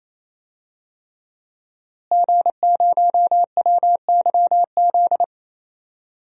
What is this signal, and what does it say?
Morse code, heard as a keyed tone, about 28 wpm, 700 Hz. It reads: G0WYZ
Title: DXCC-100 Call Signs - 5 Characters - Repeated 3x - 28wpm